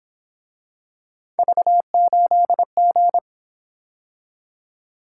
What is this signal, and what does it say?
Morse code, heard as a keyed tone, about 27 wpm, 700 Hz. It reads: V8G
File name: Speed-Racing - Three Letter-Number Combinations - 22wpm